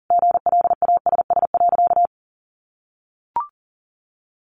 Morse code, heard as a keyed tone, about 40 wpm, 700 Hz. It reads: GLASS.
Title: Sentences from Top 1300 Words 40wpm